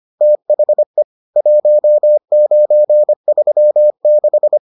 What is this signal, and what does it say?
Morse code, heard as a keyed tone, about 25 wpm, 600 Hz. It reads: THE 1936